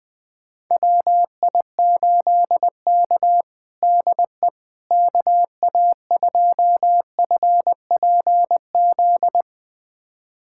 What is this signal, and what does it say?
Morse code, heard as a keyed tone, about 21 wpm, 700 Hz. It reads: WI8K DE KA2FPZ
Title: USA Callsign DE Callsign 20wpm